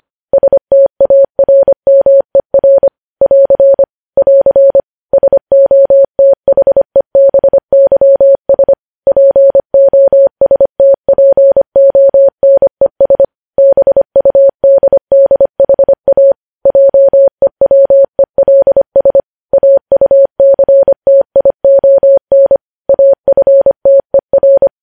Morse code, heard as a keyed tone, about 25 wpm, 550 Hz. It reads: STARMER <AR> <AR> SOTHEBYS POSTPONES BUDDHA JEWELS AUCTION AFTER
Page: news headlines in morse code at 25 wpm